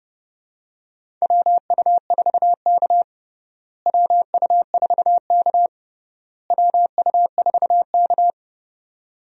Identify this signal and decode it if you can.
Morse code, heard as a keyed tone, about 30 wpm, 700 Hz. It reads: WU4K WU4K WU4K
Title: USA Call Signs - 4 Characters - Repeated 3x - 30wpm